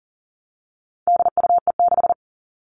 Morse code, heard as a keyed tone, about 40 wpm, 700 Hz. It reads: DUE6
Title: All Cut Numbers - 4-Digit 40wpm